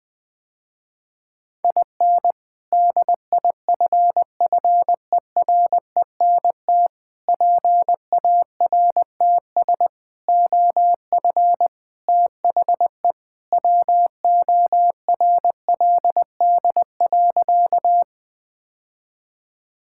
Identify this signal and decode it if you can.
Morse code, heard as a keyed tone, about 21 wpm, 700 Hz. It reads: IN DIFFERENT PARTS OF THE WORLD.
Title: Sentences with 3-Word Phrases - Spoken Beginning - 20wpm